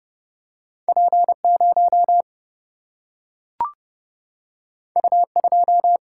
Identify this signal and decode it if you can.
Morse code, heard as a keyed tone, about 32 wpm, 700 Hz. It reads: P 0 U 2
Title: Single-Letter Single-Number 30wpm